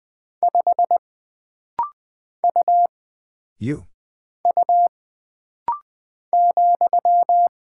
Morse code, heard as a keyed tone, about 20 wpm, 700 Hz. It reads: 5 U U ,
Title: Single Character (Letter:Number:Punctuation:Prosign) - Rapid-Fire - 20wpm